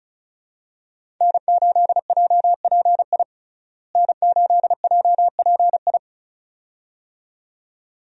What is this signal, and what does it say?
Morse code, heard as a keyed tone, about 36 wpm, 700 Hz. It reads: N8JPI N8JPI
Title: USA Call Signs - 5 Characters - Repeated 2x - 35wpm